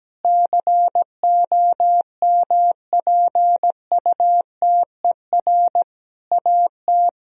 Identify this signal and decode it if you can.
Morse code, heard as a keyed tone, about 17 wpm, 700 Hz. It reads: COMPUTER AT
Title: Sentences from Top 1000 Words - Review Part 3 - 17wpm